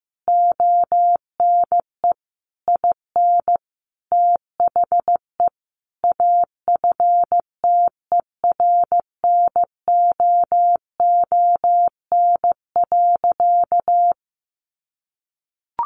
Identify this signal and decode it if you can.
Morse code, heard as a keyed tone, about 15 wpm, 700 Hz. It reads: ONE IN THE AFTERNOON.
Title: Sentences from Top 1500 Words 15wpm